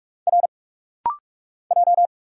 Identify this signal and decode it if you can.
Morse code, heard as a keyed tone, about 46 wpm, 700 Hz. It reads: R J
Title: Single Letters - Rapid-Fire 45wpm